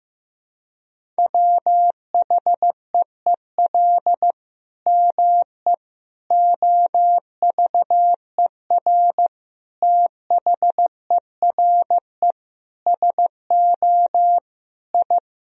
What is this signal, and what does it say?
Morse code, heard as a keyed tone, about 15 wpm, 700 Hz. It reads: WHEEL ME OVER THERE SO I
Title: Sentences from Top 1000 Words - Review Part 3 - 15wpm